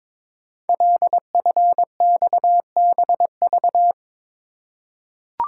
Morse code, heard as a keyed tone, about 22 wpm, 700 Hz. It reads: LFXBV
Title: Code Groups - 5 Letters - 22wpm